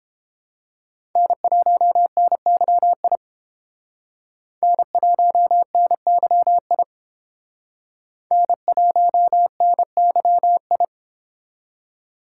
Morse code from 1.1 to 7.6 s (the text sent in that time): N1NYI N1NYI